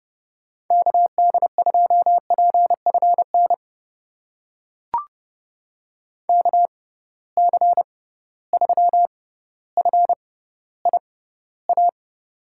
Morse code, heard as a keyed tone, about 30 wpm, spaced as for 15 wpm, 700 Hz. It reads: KD2PFN KC3FIA
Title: USA Call Signs - 6 Characters - 6x Character Spacing - 30wpm